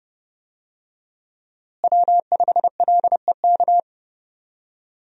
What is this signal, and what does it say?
Morse code, heard as a keyed tone, about 30 wpm, 700 Hz. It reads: W5LEK
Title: USA Call Signs - 5 Characters - Repeated 3x - 30wpm